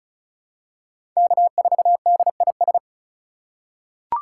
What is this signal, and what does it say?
Morse code, heard as a keyed tone, about 34 wpm, 700 Hz. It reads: K4DIS
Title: USA Call Signs - 5 Characters - 3x Character Spacing - 35wpm